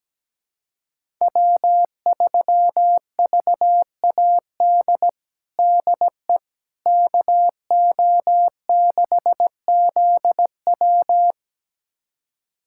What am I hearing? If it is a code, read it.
Morse code, heard as a keyed tone, about 17 wpm, 700 Hz. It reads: W3VAD DE KO6ZW
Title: USA Callsign DE Callsign - 5 Characters - 17wpm